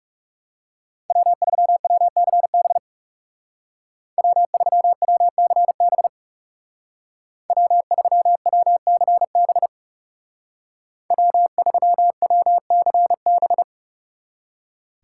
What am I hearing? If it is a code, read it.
Morse code, heard as a keyed tone, about 35 wpm, 700 Hz. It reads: W3WCB W3WCB W3WCB W3WCB
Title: Speed Racing - USA 1x3 Call Signs 30wpm